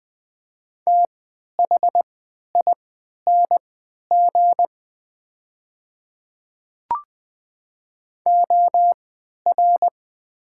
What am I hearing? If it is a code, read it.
Morse code, heard as a keyed tone, about 20 wpm, spaced as for 11 wpm, 700 Hz. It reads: THING OR
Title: 3x Character Spacing - Top 100 Words - 20wpm